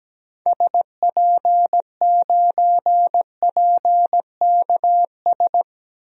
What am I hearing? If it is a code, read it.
Morse code, heard as a keyed tone, about 17 wpm, 700 Hz. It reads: SP9PKS
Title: DXCC-100 Call Signs - 6 Characters - Repeated 2x - 17wpm